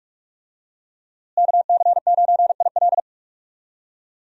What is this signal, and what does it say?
Morse code, heard as a keyed tone, about 46 wpm, 700 Hz. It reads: KC9IL
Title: USA Call Signs - 5 Characters - Repeated 2x - 45wpm